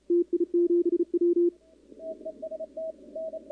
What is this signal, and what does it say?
Morse code, heard as a keyed tone, about 34 wpm, 340 Hz. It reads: TI7W
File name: contests / 2019-cw-wpx-kt5j-k5pi- -2330 0-11